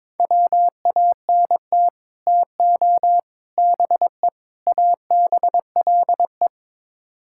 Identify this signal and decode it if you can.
Morse code, heard as a keyed tone, about 22 wpm, 700 Hz. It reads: WANT TO BE ABLE
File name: practice / 22wpm / Build A Phrase - 5-Word Phrases - 22wpm